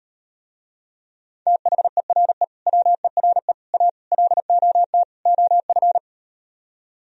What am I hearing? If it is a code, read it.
Morse code, heard as a keyed tone, about 38 wpm, 700 Hz. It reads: THERE WERE A LOT OF